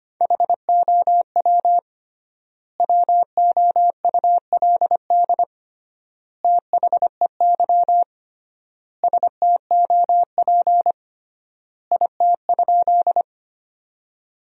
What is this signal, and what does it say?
Morse code, heard as a keyed tone, about 25 wpm, 700 Hz. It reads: HOW WOULD THEY STOP IT?
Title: 3x Word-Spacing - Sentences from Top 300 Words - 25wpm